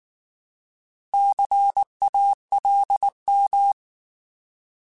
Morse code, heard as a keyed tone, about 19 wpm, 800 Hz. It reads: CALM
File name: Alphabetical / Alphabetical mp3s / Cs @18